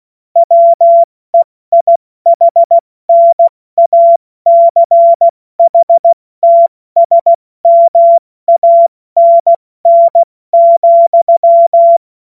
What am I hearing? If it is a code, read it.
Morse code, heard as a keyed tone, about 16 wpm, 680 Hz. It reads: WEIHNACHTSMANN,
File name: Morsekurs / Witze / Witz 18. April 70 F80